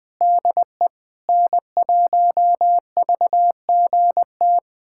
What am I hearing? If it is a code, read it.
Morse code, heard as a keyed tone, about 20 wpm, 700 Hz. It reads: DE N1VGT
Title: USA Callsign DE Callsign - 5 Characters - 20wpm